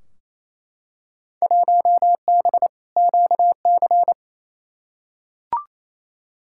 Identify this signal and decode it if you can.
Morse code, heard as a keyed tone, about 28 wpm, 700 Hz. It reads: 1B QC
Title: Field Day Exchanges Encore - 28wpm